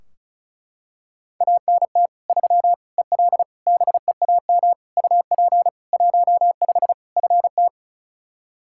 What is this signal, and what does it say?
Morse code, heard as a keyed tone, about 34 wpm, 700 Hz. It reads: ANT 3 EL BEAM UP 15 FT
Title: Practice QSO - Exchange 2 - Info - 35wpm